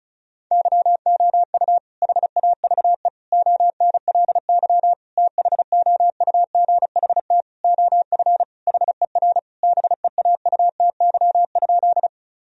Morse code, heard as a keyed tone, about 36 wpm, 700 Hz. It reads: YOU HAVE ONLY THOUGHT OF HER BEAUTY?
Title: Sentences from Top 1000 Words - Review Part 2 - 35wpm